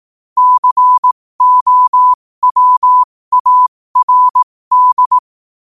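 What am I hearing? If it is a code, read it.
Morse code, heard as a keyed tone, about 18 wpm, spaced as for 15 wpm, 1000 Hz. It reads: COWARD
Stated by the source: Morse quote-of-the-day practice at 15 WPM from Morse Resource.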